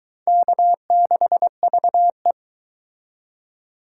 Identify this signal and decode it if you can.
Morse code, heard as a keyed tone, about 21 wpm, 700 Hz. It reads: K6VE